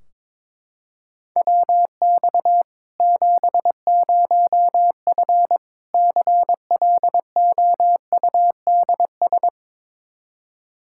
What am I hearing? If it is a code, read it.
Morse code, heard as a keyed tone, about 22 wpm, 700 Hz. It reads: WX 70F CLOUDS
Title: K1USN SST QSO Practice 22wpm